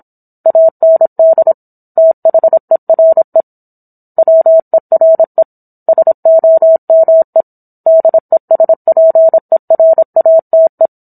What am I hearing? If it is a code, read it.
Morse code, heard as a keyed tone, about 27 wpm, 650 Hz. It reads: AND THERE WERE SOME DESPERATE